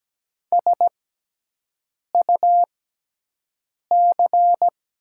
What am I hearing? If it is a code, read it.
Morse code, heard as a keyed tone, about 17 wpm, spaced as for 6 wpm, 690 Hz. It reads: SUC
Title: 6x Character Spacing - Top 300 Words - 17wpm